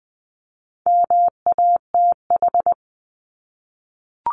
Morse code, heard as a keyed tone, about 20 wpm, 700 Hz. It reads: MATH
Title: 260 Lesson 42e - Words 20-20wpm